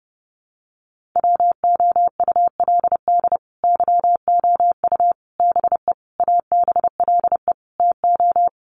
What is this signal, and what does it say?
Morse code, heard as a keyed tone, about 30 wpm, 700 Hz. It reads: WOULD YOU BE ABLE TO
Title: Sentences from Top 1400 Words 30wpm